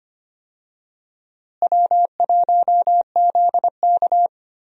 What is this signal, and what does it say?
Morse code, heard as a keyed tone, about 25 wpm, 700 Hz. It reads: W1ZK
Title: USA Call Signs - 4 Characters - Repeated 3x - 25wpm